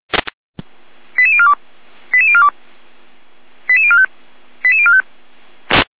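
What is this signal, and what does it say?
Five-tone selective call (ZVEI), heard as quick runs of five tones, tones of about 70 ms. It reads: 88042 88042 88045 88045
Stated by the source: ZVEI five-tone call received by SDR.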